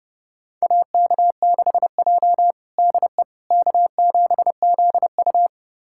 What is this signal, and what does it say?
Morse code, heard as a keyed tone, about 30 wpm, 700 Hz. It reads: AK6J DE K7ZU
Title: USA Callsign DE Callsign - 4 Characters - 30wpm